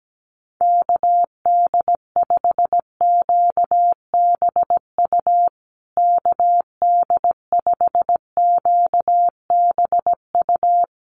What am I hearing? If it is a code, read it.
Morse code, heard as a keyed tone, about 17 wpm, 700 Hz. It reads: KD5QBU KD5QBU
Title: USA Call Signs Repeated 3x 17wpm